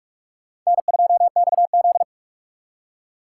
Morse code, heard as a keyed tone, about 44 wpm, 700 Hz. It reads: N2XZ